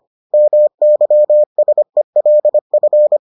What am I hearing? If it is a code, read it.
Morse code, heard as a keyed tone, about 25 wpm, 600 Hz. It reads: MYSELF